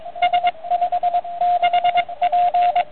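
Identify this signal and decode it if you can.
Morse code, heard as a keyed tone, about 23 wpm, 720 Hz, with radio noise. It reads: S56P
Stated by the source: CW from a QRP QCX kit transceiver on 80 m, 5 W.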